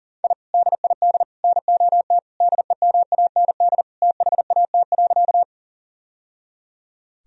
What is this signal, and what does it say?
Morse code, heard as a keyed tone, about 40 wpm, 700 Hz. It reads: I DID NOT DEMAND THAT.
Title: Sentences from Top 1000 Words - Review Part 4 - 40wpm